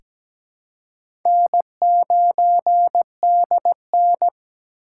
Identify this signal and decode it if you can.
Morse code, heard as a keyed tone, about 17 wpm, 700 Hz. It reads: N9DN
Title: USA Call Signs - 4 Characters - 4x Character Spacing - 17wpm